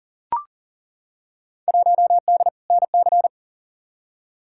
Morse code, heard as a keyed tone, about 40 wpm, 700 Hz. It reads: 1D NC
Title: Field Day Exchanges - 40wpm